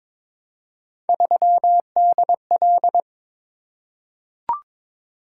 Morse code, heard as a keyed tone, about 22 wpm, 700 Hz. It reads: 3DL
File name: practice / 22wpm / Three Letter-Number Combinations - 22wpm